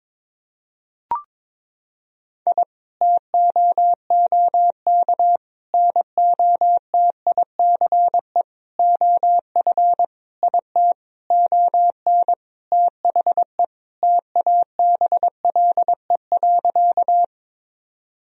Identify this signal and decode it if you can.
Morse code, heard as a keyed tone, about 22 wpm, 700 Hz. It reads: I TOOK NOTICE OF IT ON THE TABLE.
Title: Sentences from Top 1000 Words - Review Part 2 - 22wpm